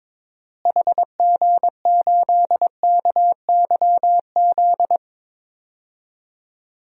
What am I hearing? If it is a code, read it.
Morse code, heard as a keyed tone, about 22 wpm, 700 Hz. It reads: HG8KYZ
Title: DXCC-100 Call Signs - 6 Characters - Repeated 2x - 22wpm